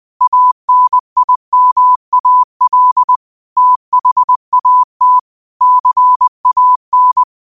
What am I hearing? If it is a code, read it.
Morse code, heard as a keyed tone, about 20 wpm, 1000 Hz. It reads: ANIMAL THAT CAN
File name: qotdfiles / 20wpm